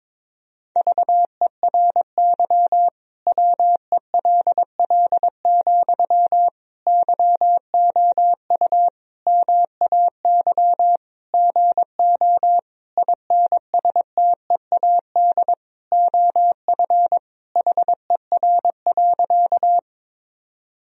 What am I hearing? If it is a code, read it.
Morse code, heard as a keyed tone, about 22 wpm, 700 Hz. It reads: VERY WELL, YOU MAY GO INSTEAD OF HER.
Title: Sentences from Top 1000 Words - Review Part 4 - 22wpm